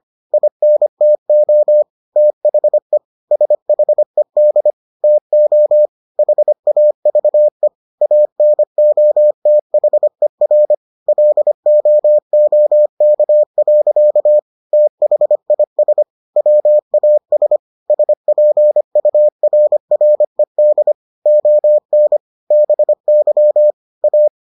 Morse code, heard as a keyed tone, about 25 wpm, 600 Hz. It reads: INTO THE SHED TO HAVE ANOTHER LOOK. THIS WAS SPURRED ON BY A